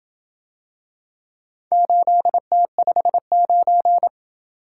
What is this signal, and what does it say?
Morse code, heard as a keyed tone, about 28 wpm, 700 Hz. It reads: 8T59